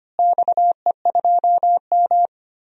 Morse code, heard as a keyed tone, about 25 wpm, 700 Hz. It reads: XE2M
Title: DXCC-100 Call Signs - 4 Characters - Repeated 3x - 25wpm